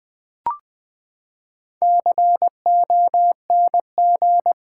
Morse code, heard as a keyed tone, about 20 wpm, 700 Hz. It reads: CONG
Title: Build A Word - 4 Letters At A Time - 12-Letter Words - 20wpm